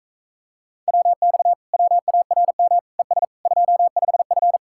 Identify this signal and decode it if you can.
Morse code, heard as a keyed tone, about 45 wpm, 700 Hz. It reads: WX WARM ES 25F